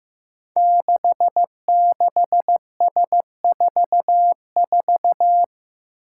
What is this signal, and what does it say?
Morse code, heard as a keyed tone, about 15 wpm, 700 Hz. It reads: 66S44
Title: Five Challenging Letter-Number Combinations 15wpm